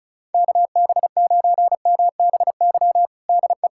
Morse code, heard as a keyed tone, about 34 wpm, 700 Hz. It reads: KB9MBY DE
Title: USA Callsign DE Callsign - 6 Characters - 35wpm